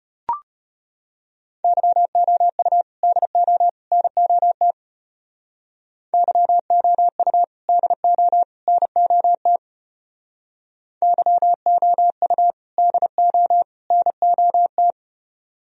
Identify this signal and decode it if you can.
Morse code, heard as a keyed tone, about 31 wpm, 700 Hz. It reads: YOU DO NOT YOU DO NOT YOU DO NOT
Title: Speed-Racing - Sets of 3 Words 25wpm